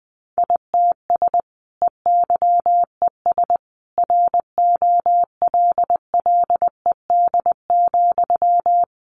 Morse code, heard as a keyed tone, about 20 wpm, 700 Hz. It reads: ITS EYES ROLLED,
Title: A Sound Of Thunder 20wpm